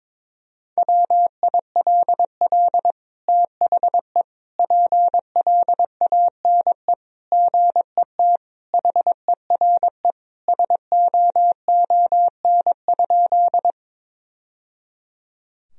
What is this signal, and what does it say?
Morse code, heard as a keyed tone, about 22 wpm, 700 Hz. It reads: WILL THE PLANE GET HERE SOON?
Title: Sentences from Top 1000 Words - Review Part 3 - 22wpm